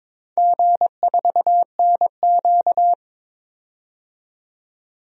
Morse code, heard as a keyed tone, about 22 wpm, 700 Hz. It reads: G4NQ
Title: DXCC-100 Call Signs - 4 Characters - 22wpm